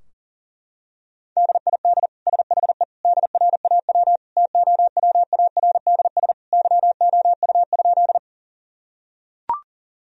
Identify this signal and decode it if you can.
Morse code, heard as a keyed tone, about 40 wpm, 700 Hz. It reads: DID SHE DRAW TOWARDS YOU?
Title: Sentences from Top 1000 Words - Review Part 2 - 40wpm